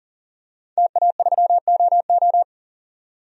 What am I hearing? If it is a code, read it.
Morse code, heard as a keyed tone, about 40 wpm, 700 Hz. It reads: TA3OO
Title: DXCC-100 Call Signs - 5 Characters - 40wpm